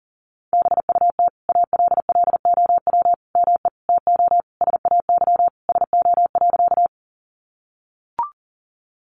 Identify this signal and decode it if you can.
Morse code, heard as a keyed tone, about 40 wpm, 700 Hz. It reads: BUT ALLOW ME TO SAY SO.
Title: Sentences from Top 700 Words 40wpm